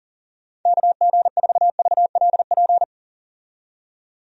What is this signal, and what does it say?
Morse code, heard as a keyed tone, about 40 wpm, 700 Hz. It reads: KG4VLP
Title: USA Call Signs - 6 Characters - 40wpm